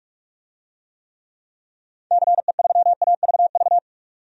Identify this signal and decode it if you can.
Morse code, heard as a keyed tone, about 44 wpm, 700 Hz. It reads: CE3AVV
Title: DXCC-100 Call Signs - 6 Characters - 45wpm